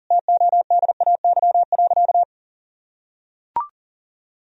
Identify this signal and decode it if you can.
Morse code, heard as a keyed tone, about 40 wpm, 700 Hz. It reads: TODAY.
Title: Sentences from Top 1000 Words - Review Part 4 - 40wpm